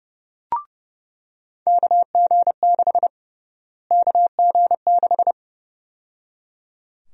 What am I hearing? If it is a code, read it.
Morse code, heard as a keyed tone, about 32 wpm, 700 Hz. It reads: KG6 KG6
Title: USA Call Sign Prefixes - Repeated 2x - 30wpm